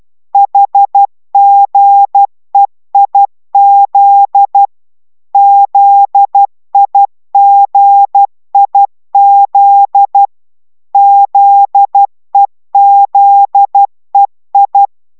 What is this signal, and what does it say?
Morse code, heard as a keyed tone, about 12 wpm, 800 Hz. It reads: HGEIZ ZIGIZ ZEZEI